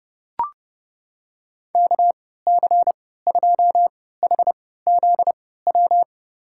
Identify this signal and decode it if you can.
Morse code, heard as a keyed tone, about 30 wpm, spaced as for 17 wpm, 700 Hz. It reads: KC2HZW
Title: USA Call Signs - 6 Characters - 3x Character Spacing - 30wpm